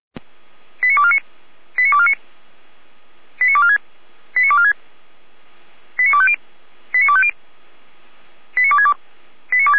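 Five-tone selective call (ZVEI), heard as quick runs of five tones, tones of about 70 ms. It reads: 89249 89249 89246 89246 89250 89250 89262
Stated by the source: ZVEI five-tone selective call received via SDR.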